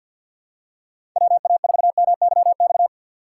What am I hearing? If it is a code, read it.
Morse code, heard as a keyed tone, about 47 wpm, 700 Hz. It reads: WA4MYX
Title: USA Call Signs - 6 Characters - 6x Character Spacing - 50wpm